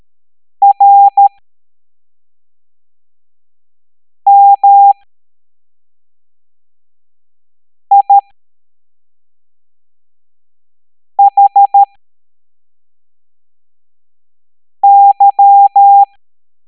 Morse code, heard as a keyed tone, about 13 wpm, spaced as for 3 wpm, 800 Hz. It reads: RMIHY
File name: hamdata / L09-EX04